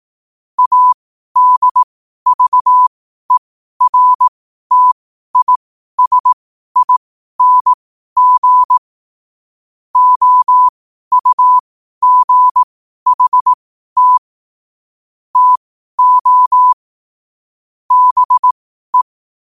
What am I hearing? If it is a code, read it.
Morse code, heard as a keyed tone, about 18 wpm, spaced as for 12 wpm, 1000 Hz. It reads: ADVERTISING OUGHT TO BE
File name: qotdfiles / 12wpm